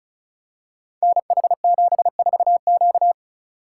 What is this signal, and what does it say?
Morse code, heard as a keyed tone, about 35 wpm, 690 Hz. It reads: NH74Q
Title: Code Groups - 5 Letter:Numbers - 35wpm